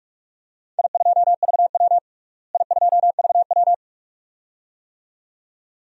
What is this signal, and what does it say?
Morse code, heard as a keyed tone, about 41 wpm, 700 Hz. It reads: I2VW I2VW
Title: DXCC-100 Call Signs - 4 Characters - Repeated 2x - 45wpm